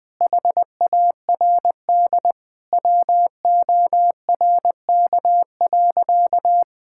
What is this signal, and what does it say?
Morse code, heard as a keyed tone, about 20 wpm, 700 Hz. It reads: HARD WORK.
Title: Sentences from Top 1000 Words - Review Part 5 - 20wpm